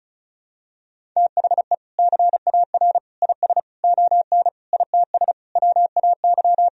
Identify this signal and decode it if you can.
Morse code, heard as a keyed tone, about 36 wpm, 700 Hz. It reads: THE CAR IS ON ITS WAY
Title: Sentences from Top 1000 Words - Review Part 1 - 35wpm